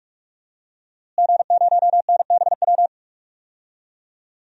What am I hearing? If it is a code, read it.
Morse code, heard as a keyed tone, about 46 wpm, 700 Hz. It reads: G0NBW
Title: DXCC-100 Call Signs - 5 Characters - Repeated 3x - 45wpm